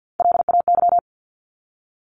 Morse code, heard as a keyed tone, about 50 wpm, 700 Hz. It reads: LAY
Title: Top 500 Words 50wpm